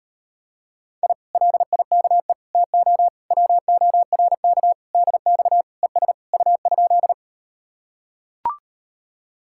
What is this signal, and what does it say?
Morse code, heard as a keyed tone, about 38 wpm, 700 Hz. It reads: I LIKE TO WORK DX ES U?